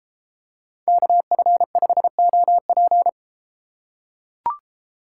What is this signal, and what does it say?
Morse code, heard as a keyed tone, about 32 wpm, 700 Hz. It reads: KF5OP